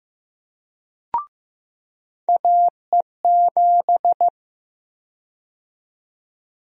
Morse code, heard as a keyed tone, about 15 wpm, 700 Hz. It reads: AE7
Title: USA Call Sign Prefixes - 15wpm